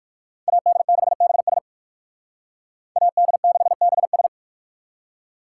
Morse code, heard as a keyed tone, about 51 wpm, 700 Hz. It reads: AD6BS AD6BS
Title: Speed Racing - USA Call Signs 45wpm